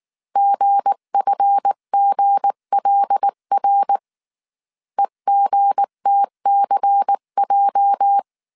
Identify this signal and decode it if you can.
Morse code, heard as a keyed tone, about 19 wpm, 800 Hz. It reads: GFGLR EGTCJ
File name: CW / DET / 1 9 1500C S B 18wpm 2551